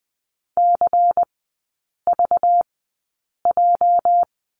Morse code, heard as a keyed tone, about 20 wpm, spaced as for 8 wpm, 700 Hz. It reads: CVJ
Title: 205 Lesson 33e - Letter j - Three Characters 20wpm